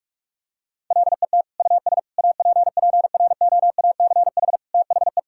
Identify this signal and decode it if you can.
Morse code, heard as a keyed tone, about 41 wpm, 700 Hz. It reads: LET US APPROACH THE
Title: Sentences from Top 1000 Words - Review Part 6 - 45wpm